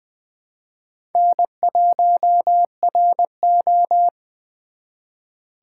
Morse code, heard as a keyed tone, about 24 wpm, 700 Hz. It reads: N1RO
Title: Speed Racing - USA 1x2 Call Signs 20wpm